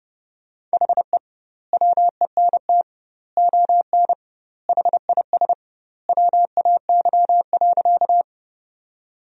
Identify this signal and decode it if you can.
Morse code, heard as a keyed tone, about 30 wpm, 700 Hz. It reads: HE WENT ON HIS WAY.
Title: 2x Word-Spacing - Sentences from Top 200 Words - 30wpm